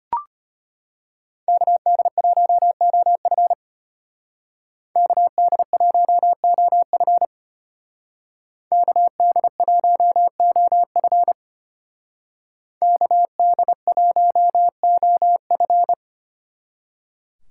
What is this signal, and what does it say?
Morse code, heard as a keyed tone, about 31 wpm, 700 Hz. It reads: KD1OF KD1OF KD1OF KD1OF
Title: Speed Racing - USA 2x2 Call Signs 25wpm